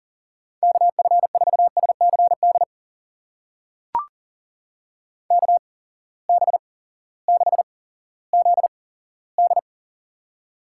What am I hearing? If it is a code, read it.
Morse code, heard as a keyed tone, about 40 wpm, spaced as for 18 wpm, 700 Hz. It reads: KF4SCD KB6ZD
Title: USA Call Signs - 8x Character Spacing - 40wpm